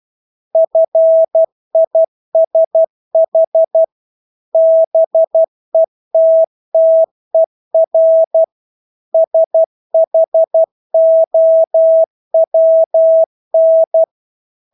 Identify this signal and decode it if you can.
Morse code, heard as a keyed tone, about 12 wpm, 650 Hz. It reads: FISH BETTER SHOWN
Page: Engelske ord 12 wpm